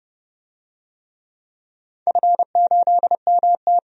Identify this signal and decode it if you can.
Morse code, heard as a keyed tone, about 30 wpm, 700 Hz. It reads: F8MT
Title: DXCC-100 Call Signs - 4 Characters - 30wpm